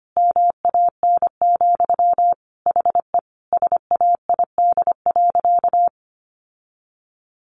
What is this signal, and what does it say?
Morse code, heard as a keyed tone, about 24 wpm, 700 Hz. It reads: MAN, HE SAID.
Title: Sentences from Top 1500 Words 25wpm